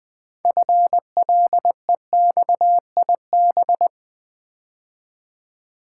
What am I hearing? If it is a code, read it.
Morse code, heard as a keyed tone, about 20 wpm, 700 Hz. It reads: FLEXIB
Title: Build A Word - 2 Letters At A Time - 8-Letter Words - 20wpm